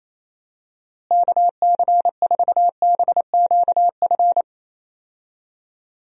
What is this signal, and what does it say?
Morse code, heard as a keyed tone, about 28 wpm, 700 Hz. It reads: KC4BQF
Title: USA Call Signs - 6 Characters - Repeated 3x - 28wpm